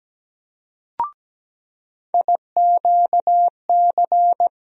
Morse code, heard as a keyed tone, about 17 wpm, 700 Hz. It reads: IQC
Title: USA Call Sign Postfixes - 3 Letters 17wpm